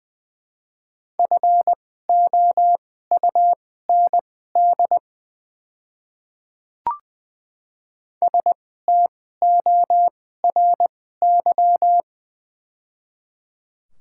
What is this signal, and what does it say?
Morse code, heard as a keyed tone, about 20 wpm, spaced as for 14 wpm, 700 Hz. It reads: FOUND STORY
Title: 2x Character Spacing - Top 300 Words - 20wpm